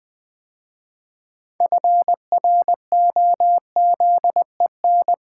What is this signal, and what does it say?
Morse code, heard as a keyed tone, about 20 wpm, 700 Hz. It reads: FROZEN
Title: 6 Letter Words - First 3 Letters Given - 20wpm